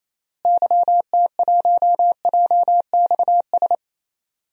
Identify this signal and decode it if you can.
Morse code, heard as a keyed tone, about 28 wpm, 700 Hz. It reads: YT1JXS
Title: DXCC-100 Call Signs - 6 Characters - Repeated 2x - 28wpm